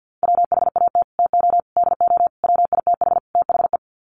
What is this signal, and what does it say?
Morse code, heard as a keyed tone, about 50 wpm, 700 Hz. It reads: WHAT TO DO WITH THE
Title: Sets of 5 Words Encore2 50wpm